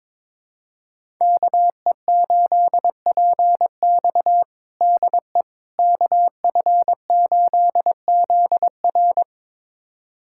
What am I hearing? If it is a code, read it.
Morse code, heard as a keyed tone, about 22 wpm, 700 Hz. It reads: KE8PX DE KF8ZR
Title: USA Callsign DE Callsign - 5 Characters - 22wpm